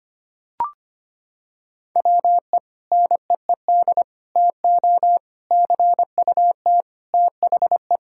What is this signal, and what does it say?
Morse code, heard as a keyed tone, about 25 wpm, 700 Hz. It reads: WE NEED TO CUT THE
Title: Sentences from Top 1000 Words - Review Part 6 - 25wpm